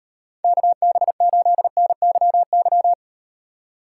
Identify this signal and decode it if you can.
Morse code, heard as a keyed tone, about 37 wpm, 700 Hz. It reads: KB8NYY